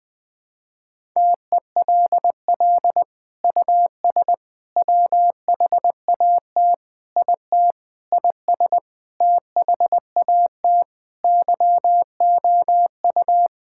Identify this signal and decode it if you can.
Morse code, heard as a keyed tone, about 20 wpm, 700 Hz. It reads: TELL US WHAT IT IS THAT YOU
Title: Sentences from Top 1000 Words - Review Part 6 - 20wpm